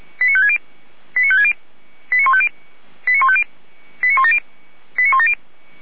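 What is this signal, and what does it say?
Five-tone selective call (ZVEI), heard as quick runs of five tones, tones of about 70 ms. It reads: 89560 89560 89150 89150 89170 89170
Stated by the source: ZVEI five-tone selective call 89167 received by SDR, from the feuerwehr-orlamuende.org fire brigade server.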